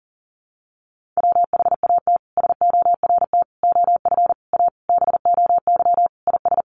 Morse code, heard as a keyed tone, about 40 wpm, 700 Hz. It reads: WHAT SORT OF A BOY IS
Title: Sentences from Top 1300 Words 40wpm